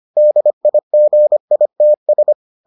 Morse code, heard as a keyed tone, about 25 wpm, 600 Hz. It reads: DIGITS